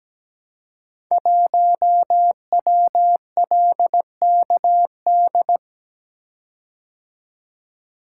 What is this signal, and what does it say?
Morse code, heard as a keyed tone, about 17 wpm, 700 Hz. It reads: 1WLKD
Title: Code Groups - 5 Letter:Numbers - 17wpm